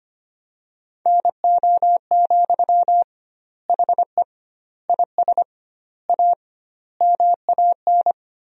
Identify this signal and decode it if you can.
Morse code, heard as a keyed tone, about 25 wpm, 700 Hz. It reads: NO, HE IS A MAN
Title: 2x Word-Spacing - Sentences from Top 400 Words - 25wpm